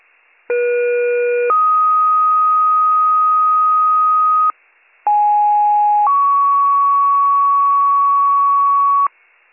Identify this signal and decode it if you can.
Two-tone page, heard as a one-second tone, then a three-second tone, about 480 then 1250 Hz, about 830 then 1120 Hz. It is heard from 0.5 to 9.1 s.